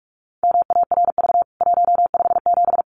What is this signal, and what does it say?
Morse code, heard as a keyed tone, about 44 wpm, 700 Hz. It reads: MARV 157
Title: CWT Exchange 45wpm